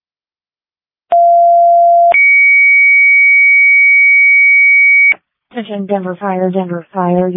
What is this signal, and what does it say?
Two-tone page, heard as a one-second tone, then a three-second tone, about 690 then 2250 Hz, with speech. It is heard from 1.1 to 5.1 s.